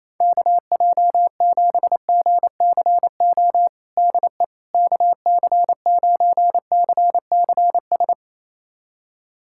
Morse code, heard as a keyed tone, about 28 wpm, 700 Hz. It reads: KJ7GCO DE KC9CCS
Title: USA Callsign DE Callsign 28wpm